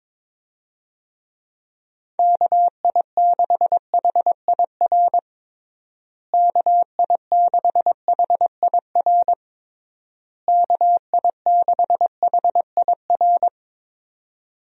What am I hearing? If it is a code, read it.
Morse code, heard as a keyed tone, about 21 wpm, 700 Hz. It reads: KI6HIR KI6HIR KI6HIR
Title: USA Call Signs - 6 Characters - Repeated 3x - 22wpm